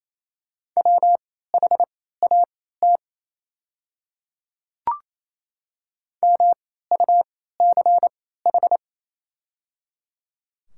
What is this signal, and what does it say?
Morse code, heard as a keyed tone, about 28 wpm, spaced as for 16 wpm, 690 Hz. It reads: WHAT MUCH
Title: 3x Character Spacing - Top 200 Words - 28wpm